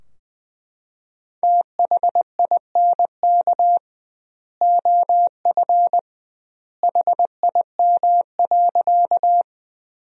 Morse code, heard as a keyed tone, about 21 wpm, 700 Hz. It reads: THINK OF HIM.
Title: 2x Word-Spacing - Sentences from Top 200 Words - 20wpm